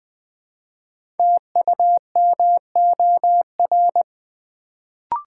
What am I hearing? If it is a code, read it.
Morse code, heard as a keyed tone, about 20 wpm, 700 Hz. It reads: TUMOR
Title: Build A Word - 1 Letter At A Time - 5-Letter Words - 20wpm